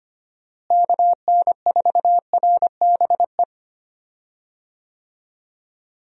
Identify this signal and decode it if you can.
Morse code, heard as a keyed tone, about 25 wpm, 700 Hz. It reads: KN4RBE
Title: USA Call Signs - 6 Characters - Repeated 3x - 25wpm